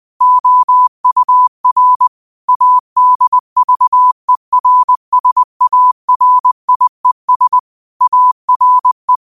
Morse code, heard as a keyed tone, about 20 wpm, 1000 Hz. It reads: OUR ADVERSARIES ARE